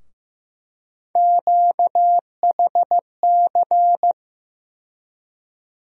Morse code, heard as a keyed tone, about 15 wpm, 700 Hz. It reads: QHC
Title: USA Call Sign Postfixes - 3 Letters 15wpm